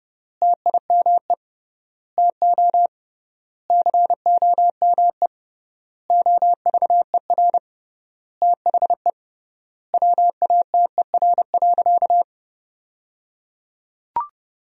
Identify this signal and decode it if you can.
Morse code, heard as a keyed tone, about 30 wpm, 700 Hz. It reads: TIME TO COME OVER THE WATER.
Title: 3x Word-Spacing - Sentences from Top 300 Words - 30wpm